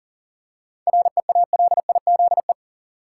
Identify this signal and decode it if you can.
Morse code, heard as a keyed tone, about 40 wpm, 700 Hz. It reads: REALIZE
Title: Seven Letter Words - 40wpm